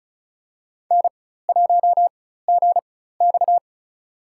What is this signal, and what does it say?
Morse code, heard as a keyed tone, about 36 wpm, spaced as for 16 wpm, 700 Hz. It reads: N1GX
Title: USA Call Signs - 4 Characters - 4x Character Spacing - 35wpm